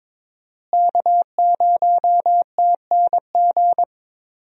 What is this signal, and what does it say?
Morse code, heard as a keyed tone, about 22 wpm, 700 Hz. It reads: K0TNG
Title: USA Call Signs - 5 Characters - Repeated 2x - 22wpm